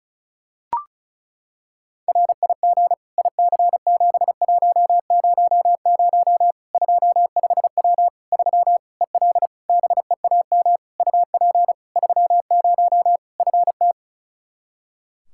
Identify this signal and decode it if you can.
Morse code, heard as a keyed tone, about 36 wpm, 700 Hz. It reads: RIG IC7100 25W 3 EL BEAM UP 30 FT
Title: Practice QSO - Exchange 2 - Info - 35wpm